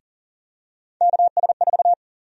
Morse code, heard as a keyed tone, about 40 wpm, 700 Hz. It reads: KS4
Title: USA Call Sign Prefixes - Repeated 2x - 40wpm